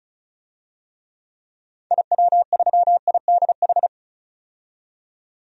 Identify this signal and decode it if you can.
Morse code, heard as a keyed tone, about 34 wpm, 700 Hz. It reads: IW3IDH
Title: DXCC-100 Call Signs - 6 Characters - 35wpm